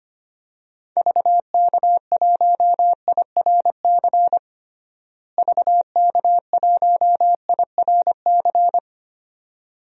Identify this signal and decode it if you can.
Morse code, heard as a keyed tone, about 25 wpm, 700 Hz. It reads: VK1IRC VK1IRC
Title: DXCC-100 Call Signs - 6 Characters - Repeated 2x - 25wpm